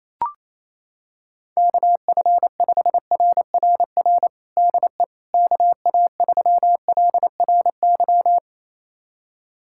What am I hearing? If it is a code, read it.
Morse code, heard as a keyed tone, about 28 wpm, 700 Hz. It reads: KF5RRR DE KA3LRY
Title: USA Callsign DE Callsign - 6 Characters - 28wpm